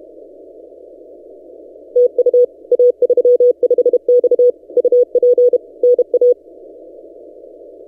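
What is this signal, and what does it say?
Morse code, heard as a keyed tone, about 30 wpm, 490 Hz. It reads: TU A35X UP NA
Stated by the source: Morse from DXpedition station A35X on 12 m, posted by N6ML.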